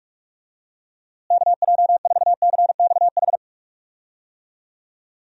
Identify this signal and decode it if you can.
Morse code, heard as a keyed tone, about 46 wpm, 700 Hz. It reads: KJ4CXH
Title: USA Call Signs - 6 Characters - Repeated 3x - 45wpm